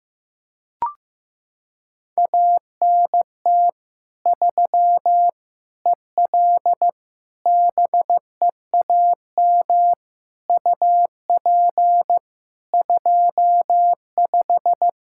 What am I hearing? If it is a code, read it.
Morse code, heard as a keyed tone, about 15 wpm, 700 Hz. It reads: ANT 3 EL BEAM UP 25
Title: Practice QSO - Exchange 2 - Info - 15wpm